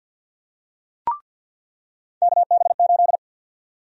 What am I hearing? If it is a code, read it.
Morse code, heard as a keyed tone, about 46 wpm, 700 Hz. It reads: KB8
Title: USA Call Sign Prefixes - 50wpm